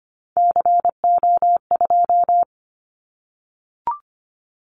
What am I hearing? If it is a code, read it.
Morse code, heard as a keyed tone, about 25 wpm, 700 Hz. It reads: CO2
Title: Callsign Prefixes for Top 100 DXCC - 25wpm